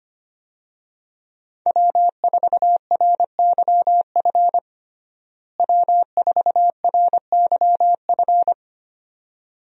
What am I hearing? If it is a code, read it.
Morse code, heard as a keyed tone, about 25 wpm, 700 Hz. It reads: W4RYF W4RYF
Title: USA Call Signs - 5 Characters - Repeated 2x - 25wpm